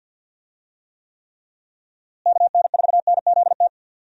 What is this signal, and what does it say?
Morse code, heard as a keyed tone, about 51 wpm, 700 Hz. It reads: KN4NZT
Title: USA Call Signs - 6 Characters - Repeated 3x - 50wpm